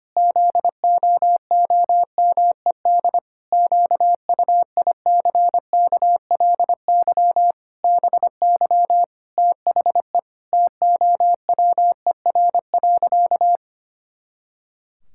Morse code, heard as a keyed tone, about 25 wpm, 700 Hz. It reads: ZOOMED QUICKLY BY THE TOWER.
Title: Pangram Challenge v2 - 25wpm